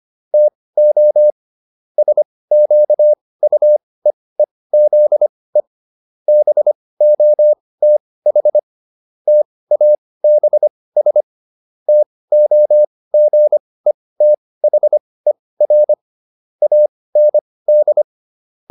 Morse code, heard as a keyed tone, about 25 wpm, spaced as for 18 wpm, 600 Hz. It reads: TO SQUEEZE BOTH TABS TOGETHER AND